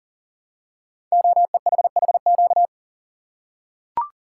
Morse code, heard as a keyed tone, about 43 wpm, 700 Hz. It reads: OEHHQ
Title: Code Groups - 5 Letters - 40wpm